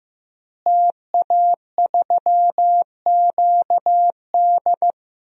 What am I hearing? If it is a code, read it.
Morse code, heard as a keyed tone, about 15 wpm, 700 Hz. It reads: TA3QD
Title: DXCC-100 Call Signs - 5 Characters - Repeated 3x - 15wpm